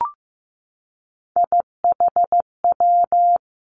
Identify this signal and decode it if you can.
Morse code, heard as a keyed tone, about 15 wpm, 700 Hz. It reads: IHW
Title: Three Challenging Letter-Number Combinations 15wpm